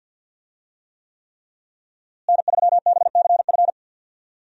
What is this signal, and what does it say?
Morse code, heard as a keyed tone, about 52 wpm, 700 Hz. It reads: N3BCF
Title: USA Call Signs - 5 Characters - Repeated 3x - 50wpm